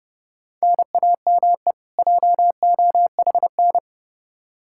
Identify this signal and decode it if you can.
Morse code, heard as a keyed tone, about 30 wpm, 700 Hz. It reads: NAME JOHN
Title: Practice QSO - Exchange 1 - Beginning - 30wpm